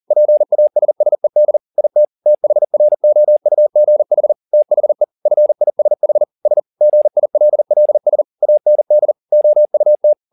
Morse code, heard as a keyed tone, about 40 wpm, 600 Hz. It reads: PASSED IT THROUGH THE FISH S GILLS AND OUT
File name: the old man and the sea 2 40wpm